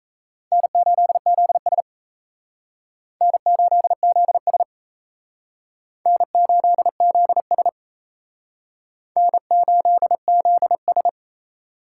Morse code, heard as a keyed tone, about 35 wpm, 700 Hz. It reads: N8ZS N8ZS N8ZS N8ZS
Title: Speed Racing - USA Call Signs 28wpm